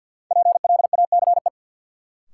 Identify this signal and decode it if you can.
Morse code, heard as a keyed tone, about 49 wpm, 700 Hz. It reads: PLACE
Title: Sets of 4 Words - New Word Drill - 50wpm